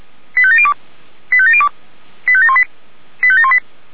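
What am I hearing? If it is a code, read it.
Five-tone selective call (ZVEI), heard as quick runs of five tones, tones of about 70 ms. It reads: 85702 85702 85718 85718
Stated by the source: Five-tone ZVEI selective call received on SDR.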